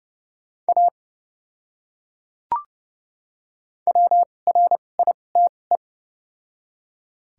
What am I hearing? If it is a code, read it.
Morse code, heard as a keyed tone, about 29 wpm, spaced as for 21 wpm, 700 Hz. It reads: A WRITE
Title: 2x Character Spacing - Top 100 Words - 30wpm